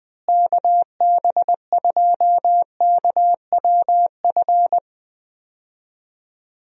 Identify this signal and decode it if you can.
Morse code, heard as a keyed tone, about 20 wpm, 700 Hz. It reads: KB2KWF
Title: USA Call Signs - 6 Characters - Repeated 2x - 20wpm